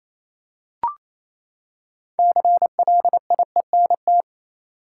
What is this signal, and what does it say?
Morse code, heard as a keyed tone, about 27 wpm, 700 Hz. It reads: CLIENT
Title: Six Letter Words - 28wpm